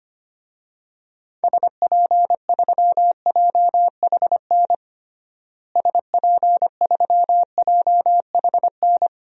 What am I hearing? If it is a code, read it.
Morse code, heard as a keyed tone, about 25 wpm, 700 Hz. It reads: SP3JHN SP3JHN
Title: DXCC-100 Call Signs - 6 Characters - Repeated 2x - 25wpm